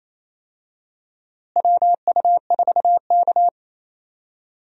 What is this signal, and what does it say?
Morse code, heard as a keyed tone, about 28 wpm, 700 Hz. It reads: WU4K
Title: USA Call Signs - 4 Characters - Repeated 3x - 28wpm